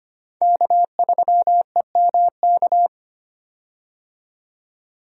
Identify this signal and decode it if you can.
Morse code, heard as a keyed tone, about 25 wpm, 700 Hz. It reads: K3EMK
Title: USA Call Signs - 5 Characters - 25wpm